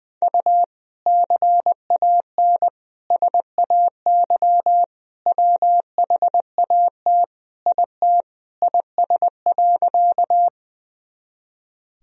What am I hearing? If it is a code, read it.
Morse code, heard as a keyed tone, about 20 wpm, 700 Hz. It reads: U CAN SAY WHAT IT IS.
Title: Sentences from Top 200 Words and Abbreviations 20wpm